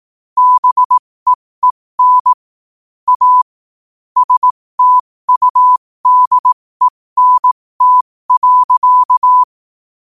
Morse code, heard as a keyed tone, about 18 wpm, spaced as for 15 wpm, 1000 Hz. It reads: BEEN A STUDENT.
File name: qotdfiles / 15wpm